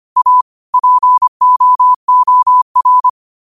Morse code, heard as a keyed tone, about 25 wpm, 1000 Hz. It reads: A POOR
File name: qotdfiles / 25wpm